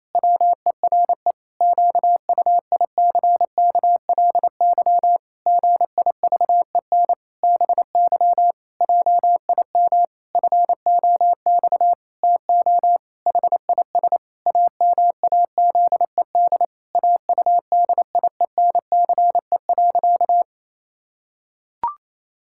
Morse code, heard as a keyed tone, about 28 wpm, 700 Hz. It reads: WERE QUICKLY GIVEN BY JIM FOX TO HIS AMAZED AUDIENCE.
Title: Pangram Challenge v2 - 28wpm